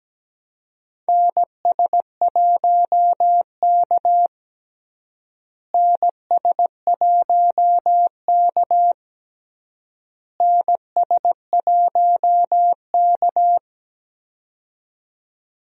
Morse code, heard as a keyed tone, about 17 wpm, 700 Hz. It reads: NS1K NS1K NS1K
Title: USA Call Signs - 4 Characters - Repeated 3x - 17wpm